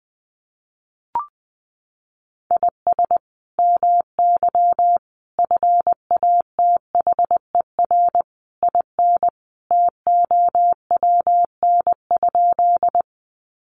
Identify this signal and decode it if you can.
Morse code, heard as a keyed tone, about 20 wpm, 700 Hz. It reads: IS MY FATHER IN TOWN?
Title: Sentences from Top 500 Words 20wpm